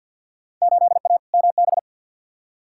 Morse code, heard as a keyed tone, about 46 wpm, 700 Hz. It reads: 8A MB
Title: Field Day Exchanges Encore - 50wpm